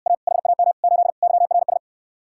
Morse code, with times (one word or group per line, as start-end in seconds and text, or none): 0.1-0.1 A
0.3-0.7 HAM
0.8-1.1 8
1.2-1.8 YRS